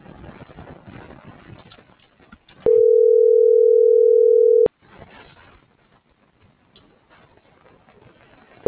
Ringback tone (North America), 440 + 480 Hz.